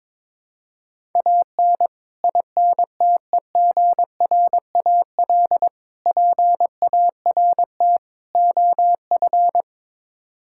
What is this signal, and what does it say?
Morse code, heard as a keyed tone, about 21 wpm, 700 Hz. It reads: AN INTEGRAL PART OF
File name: practice / 22wpm / Build A Phrase - 5-Word Phrases - 22wpm